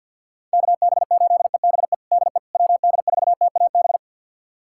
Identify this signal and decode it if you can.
Morse code, heard as a keyed tone, about 51 wpm, 690 Hz. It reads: KB8EBE DE WD4TAB
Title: USA Callsign DE Callsign - 6 Characters - 50wpm